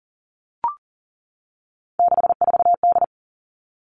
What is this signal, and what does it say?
Morse code, heard as a keyed tone, about 42 wpm, 700 Hz. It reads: 64D
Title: Three Challenging Letter-Number Combinations 40wpm